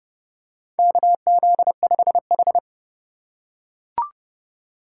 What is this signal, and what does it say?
Morse code, heard as a keyed tone, about 30 wpm, 700 Hz. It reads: KZ5H
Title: USA Call Signs - 4 Characters - 3x Character Spacing - 30wpm